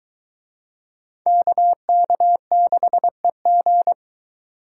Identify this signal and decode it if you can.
Morse code, heard as a keyed tone, about 20 wpm, 700 Hz. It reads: KK6EG